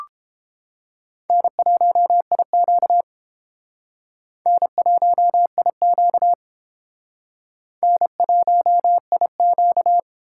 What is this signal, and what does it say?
Morse code, heard as a keyed tone, about 30 wpm, 700 Hz. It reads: N1IQ N1IQ N1IQ